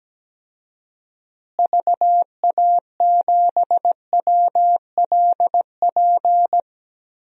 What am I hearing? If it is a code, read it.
Morse code, heard as a keyed tone, about 17 wpm, 700 Hz. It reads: VA7WLP
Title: DXCC-100 Call Signs - 6 Characters - Repeated 2x - 17wpm